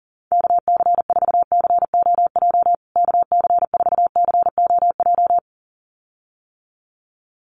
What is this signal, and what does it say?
Morse code, heard as a keyed tone, about 40 wpm, 700 Hz. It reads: KC4COJ KC4COJ
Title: USA Call Signs Repeated 2x 40wpm